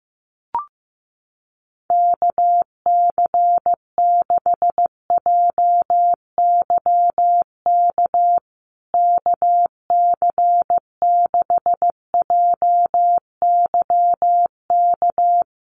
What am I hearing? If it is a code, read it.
Morse code, heard as a keyed tone, about 15 wpm, 700 Hz. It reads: KC6JYK KC6JYK
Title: USA Call Signs Repeated 3x 15wpm